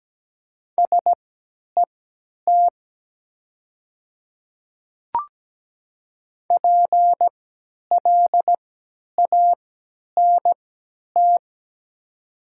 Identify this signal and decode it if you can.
Morse code, heard as a keyed tone, about 17 wpm, spaced as for 10 wpm, 700 Hz. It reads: SET PLANT
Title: 3x Character Spacing - Top 500 Words - 17wpm